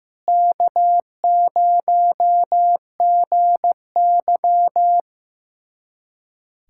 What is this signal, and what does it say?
Morse code, heard as a keyed tone, about 15 wpm, 700 Hz. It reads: K0GY
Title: USA Callsign DE Callsign - 4 Characters - 15wpm